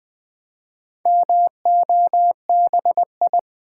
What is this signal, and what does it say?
Morse code, heard as a keyed tone, about 20 wpm, 700 Hz. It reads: MOBI
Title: Build A Word - 4 Letters At A Time - 8-Letter Words - 20wpm